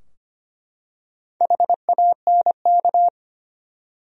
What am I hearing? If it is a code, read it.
Morse code, heard as a keyed tone, about 25 wpm, 700 Hz. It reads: HANK
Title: USA Common Names - 25wpm